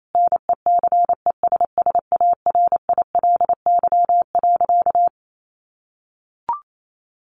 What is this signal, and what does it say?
Morse code, heard as a keyed tone, about 28 wpm, 700 Hz. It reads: NECESSARILY.
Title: Sentences from Top 1400 Words 28wpm